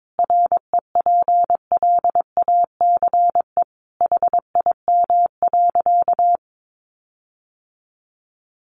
Morse code, heard as a keyed tone, about 22 wpm, 700 Hz. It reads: REPLACE HIM.
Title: Sentences from Top 1400 Words 22wpm